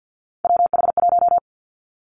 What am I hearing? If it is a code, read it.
Morse code, heard as a keyed tone, about 50 wpm, 700 Hz. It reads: WH1
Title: Three Challenging Letter-Number Combinations 50wpm